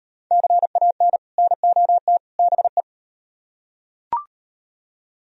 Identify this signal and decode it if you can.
Morse code, heard as a keyed tone, about 36 wpm, 700 Hz. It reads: CAN NOT BE